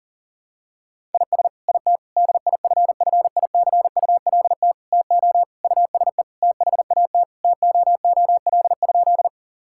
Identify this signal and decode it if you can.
Morse code, heard as a keyed tone, about 40 wpm, 700 Hz. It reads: IS IT DIFFICULT TO USE THAT TOOL?
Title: Sentences from Top 1000 Words - Review Part 6 - 40wpm